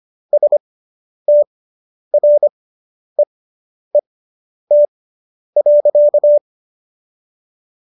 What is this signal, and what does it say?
Morse code, heard as a keyed tone, about 25 wpm, spaced as for 10 wpm, 590 Hz. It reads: STREET.